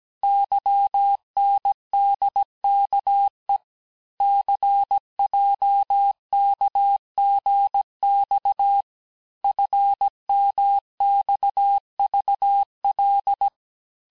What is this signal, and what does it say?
Morse code, heard as a keyed tone, about 17 wpm, 800 Hz. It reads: YNDKE CJKGX FMXVL
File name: cours-du-lundi files / corrige-cours-1581-texte-chiffre-vitesse-16-mpm